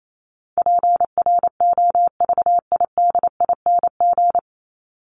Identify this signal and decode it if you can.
Morse code, heard as a keyed tone, about 28 wpm, 700 Hz. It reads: PROVIDING
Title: ING Suffix - Top 100 Words with Base Words 28wpm